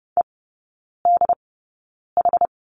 Morse code, heard as a keyed tone, about 30 wpm, spaced as for 9 wpm, 700 Hz. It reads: EDH
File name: course / 30:10wpm / 087 Lesson 15e - Number 5 - Three Characters 30wpm